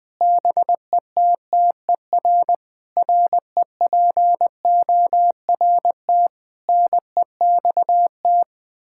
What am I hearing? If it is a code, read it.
Morse code, heard as a keyed tone, about 20 wpm, 700 Hz. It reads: BETTER REPORT NEXT
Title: Sentences from Top 1000 Words - Review Part 6 - 20wpm